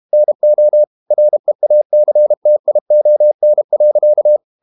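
Morse code, heard as a keyed tone, about 32 wpm, 600 Hz. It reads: NO REACTION.
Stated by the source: Morse code practice at 32 wpm from CW med Gnister, reading Ken Follett's novel Never in English.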